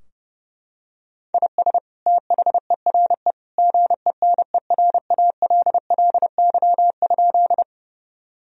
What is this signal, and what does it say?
Morse code, heard as a keyed tone, about 30 wpm, 700 Hz. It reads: IS THERE GENERALLY?
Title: Sentences from Top 1000 Words - Review Part 4 - 30wpm